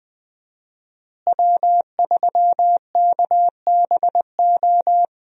Morse code, heard as a keyed tone, about 20 wpm, 700 Hz. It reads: W3KBO
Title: USA Call Signs - 5 Characters - Repeated 3x - 20wpm